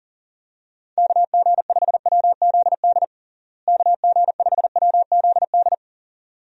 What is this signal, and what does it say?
Morse code, heard as a keyed tone, about 37 wpm, 700 Hz. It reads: KG5WZD KG5WZD
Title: USA Call Signs - 6 Characters - Repeated 2x - 40wpm